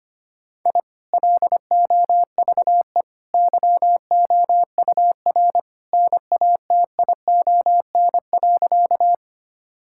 Morse code, heard as a keyed tone, about 25 wpm, 700 Hz. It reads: I LOVE YOUR NATION.
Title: Sentences from Top 1000 Words - Review Part 6 - 25wpm